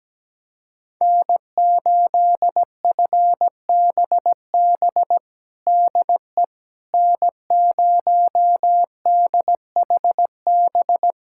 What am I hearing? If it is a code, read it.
Morse code, heard as a keyed tone, about 17 wpm, 700 Hz. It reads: N8FBB DE N0DHB
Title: USA Callsign DE Callsign - 5 Characters - 17wpm